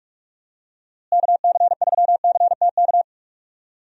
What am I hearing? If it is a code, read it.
Morse code, heard as a keyed tone, about 46 wpm, 700 Hz. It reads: KC3CTK
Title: USA Call Signs - 6 Characters - Repeated 3x - 45wpm